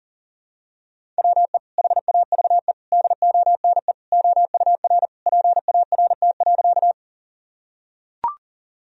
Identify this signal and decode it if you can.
Morse code, heard as a keyed tone, about 40 wpm, 700 Hz. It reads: WE HAVE DONE OUR PART.
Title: Sentences from Top 1000 Words - Review Part 1 - 40wpm